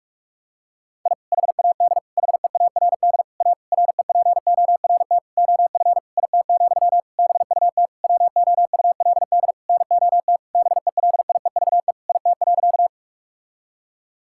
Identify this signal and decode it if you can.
Morse code, heard as a keyed tone, about 46 wpm, 700 Hz. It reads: I HAD HEARD A REPORT OF IT, BUT WOULD NOT BELIEVE IT.
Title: Sentences from Top 1000 Words - Review Part 6 - 45wpm